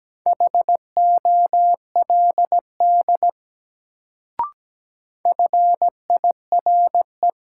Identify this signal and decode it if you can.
Morse code, heard as a keyed tone, about 17 wpm, 700 Hz. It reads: HOLD FIRE
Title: Sentences from Top 400 Words - New Word Drill - 17wpm